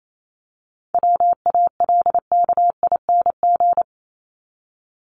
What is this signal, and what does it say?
Morse code, heard as a keyed tone, about 28 wpm, 700 Hz. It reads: WALKING
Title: ING Suffix - Top 100 Words with Base Words 28wpm